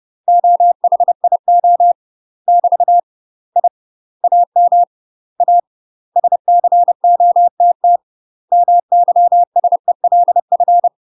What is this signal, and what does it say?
Morse code, heard as a keyed tone, about 30 wpm, 700 Hz. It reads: OHIO <BT> I AM A SCOTT MYSELF
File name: UK MASTER QSOs TRAINING / 30WPM / UK MASTER QSOs 0017